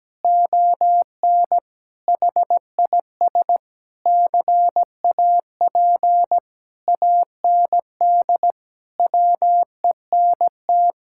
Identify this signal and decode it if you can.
Morse code, heard as a keyed tone, about 17 wpm, 700 Hz. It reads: ON HIS CAP AND WENT
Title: Sentences from Top 1600 Words 17wpm